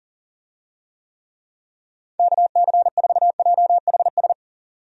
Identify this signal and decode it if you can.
Morse code, heard as a keyed tone, about 40 wpm, 700 Hz. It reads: KC4JHS
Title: USA Call Signs - 6 Characters - 40wpm